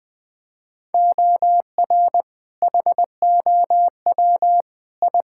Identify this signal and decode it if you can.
Morse code, heard as a keyed tone, about 21 wpm, 700 Hz. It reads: OR HOW I
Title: Sentences with 8-Word Phrases - Spoken Beginning - 20wpm